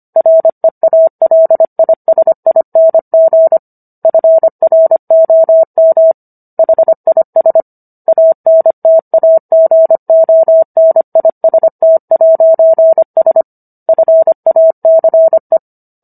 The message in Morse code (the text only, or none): REALISING FROM HIS ANTAGONIST'S FACE